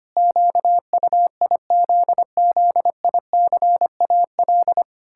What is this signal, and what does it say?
Morse code, heard as a keyed tone, about 25 wpm, 700 Hz. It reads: QUIZZICAL
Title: Pangram Challenge v2 - 25wpm